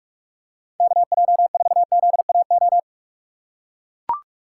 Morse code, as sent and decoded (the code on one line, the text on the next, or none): -.- .--- ....- --.. .- ---
KJ4ZAO